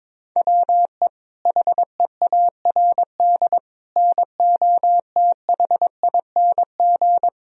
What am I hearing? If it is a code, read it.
Morse code, heard as a keyed tone, about 22 wpm, 700 Hz. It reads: WE HEARD NOTHING
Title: Sentences from Top 1000 Words - Review Part 3 - 22wpm